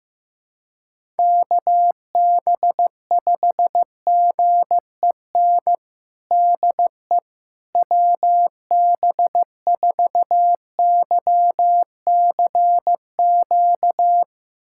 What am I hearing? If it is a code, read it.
Morse code, heard as a keyed tone, about 15 wpm, 700 Hz. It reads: KB5GEN DE WB4YCQ
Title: USA Callsign DE Callsign - 6 Characters - 15wpm